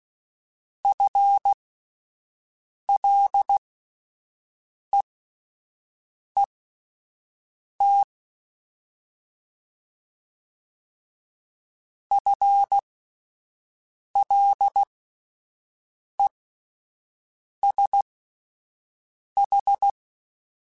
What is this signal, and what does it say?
Morse code, heard as a keyed tone, about 16 wpm, spaced as for 5 wpm, 800 Hz. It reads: FLEET FLESH
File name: Alphabetical / Alphabetical mp3s / Fs @5